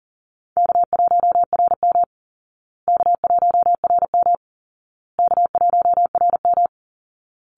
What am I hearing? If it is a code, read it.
Morse code, heard as a keyed tone, about 40 wpm, 700 Hz. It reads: K1RM K1RM K1RM
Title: Easy Call Signs Repeated 3x - 40wpm